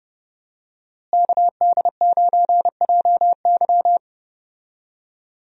Morse code, heard as a keyed tone, about 31 wpm, 700 Hz. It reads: KD9JY
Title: USA Call Signs - 5 Characters - Repeated 2x - 30wpm